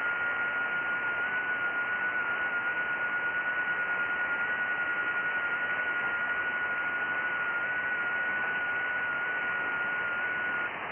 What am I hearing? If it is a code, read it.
FT8.
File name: IC-7100